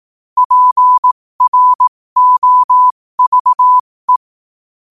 Morse code, heard as a keyed tone, about 18 wpm, spaced as for 15 wpm, 1000 Hz. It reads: PROVE
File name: qotdfiles / 15wpm